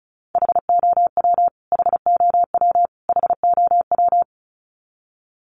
Morse code, heard as a keyed tone, about 34 wpm, 700 Hz. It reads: HOW HOW HOW
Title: Top 500 Words Repeated 35wpm